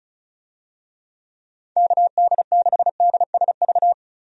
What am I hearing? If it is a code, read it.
Morse code, heard as a keyed tone, about 34 wpm, 700 Hz. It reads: KD6DSV
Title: USA Call Signs - 6 Characters - 35wpm